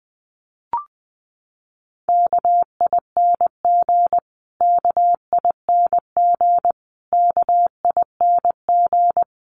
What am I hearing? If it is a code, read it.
Morse code, heard as a keyed tone, about 20 wpm, 700 Hz. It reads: KING KING KING
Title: Top 500 Words Repeated 20wpm